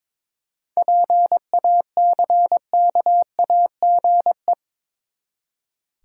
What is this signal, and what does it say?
Morse code, heard as a keyed tone, about 22 wpm, 700 Hz. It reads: PACKAGE
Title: Seven Letter Words - 22wpm